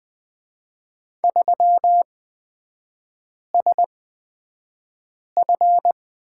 Morse code, heard as a keyed tone, about 20 wpm, spaced as for 5 wpm, 700 Hz. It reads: 3SF
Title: 224 Lesson 36e - BT - Three Characters 20wpm